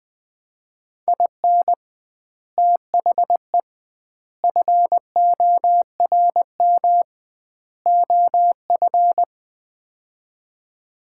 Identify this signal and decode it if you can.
Morse code, heard as a keyed tone, about 21 wpm, 700 Hz. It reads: IN THE FORM OF
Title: 2x Word-Spacing - Sets of 4 Words - 20wpm